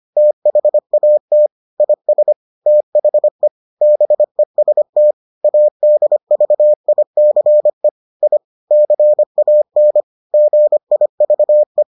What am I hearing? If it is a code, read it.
Morse code, heard as a keyed tone, about 25 wpm, 600 Hz. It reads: THAT IS THE BEST ADVICE I CAN GIVE